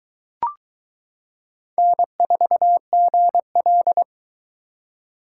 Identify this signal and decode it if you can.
Morse code, heard as a keyed tone, about 22 wpm, 700 Hz. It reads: N4GL